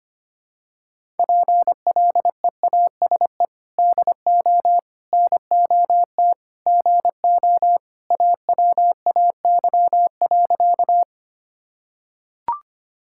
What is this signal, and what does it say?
Morse code, heard as a keyed tone, about 25 wpm, 700 Hz. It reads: PLEASE DO NOT GO AWAY.
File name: practice / 25wpm / Sentences with 4-Word Phrases - 25wpm